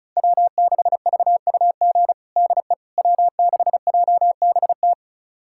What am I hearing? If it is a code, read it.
Morse code, heard as a keyed tone, about 34 wpm, 700 Hz. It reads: W6VUG DE W6JBT
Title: USA Callsign DE Callsign - 5 Characters - 35wpm